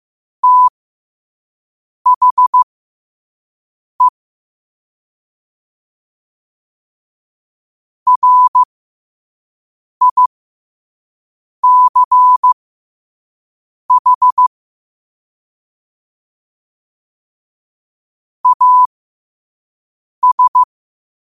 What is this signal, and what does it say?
Morse code, heard as a keyed tone, about 15 wpm, spaced as for 5 wpm, 1000 Hz. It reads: THE RICH AS
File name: qotdfiles / 5wpm